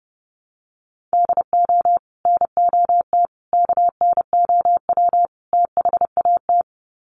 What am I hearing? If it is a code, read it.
Morse code, heard as a keyed tone, about 30 wpm, 700 Hz. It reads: DO NOT KNOW THAT
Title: Sets of 4 Words 30wpm